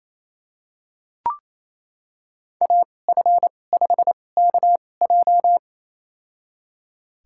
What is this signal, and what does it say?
Morse code, heard as a keyed tone, about 28 wpm, spaced as for 20 wpm, 700 Hz. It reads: AF5KJ
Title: USA Call Signs - 5 Characters - 2x Character Spacing - 28wpm